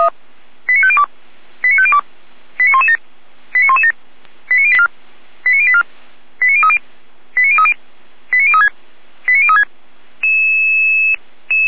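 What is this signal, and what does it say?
Five-tone selective call (ZVEI), heard as quick runs of five tones, tones of about 70 ms. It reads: 89502 89502 89117 89117 89084 89030 89030 89036 89036
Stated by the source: ZVEI five-tone selective call 89086, received with an SDR, archived by Feuerwehr Orlamünde.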